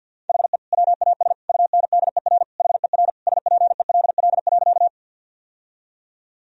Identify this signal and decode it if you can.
Morse code, heard as a keyed tone, about 47 wpm, 700 Hz. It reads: HE WAS UNDER HER SPELL.
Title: Sentences from Top 1000 Words - Review Part 1 - 50wpm